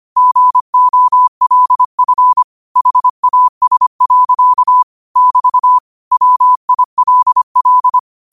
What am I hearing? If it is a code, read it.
Morse code, heard as a keyed tone, about 25 wpm, 1000 Hz. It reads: GOLF HAS. <BT> WILL